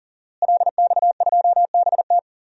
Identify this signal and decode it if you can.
Morse code, heard as a keyed tone, about 40 wpm, 700 Hz. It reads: LX2BT
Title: DXCC-100 Call Signs - 5 Characters - Repeated 2x - 40wpm